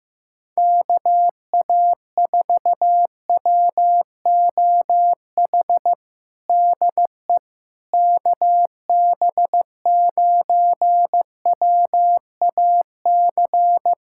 Morse code, heard as a keyed tone, about 15 wpm, 700 Hz. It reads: KA4WOH DE KB9WAC
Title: USA Callsign DE Callsign 15wpm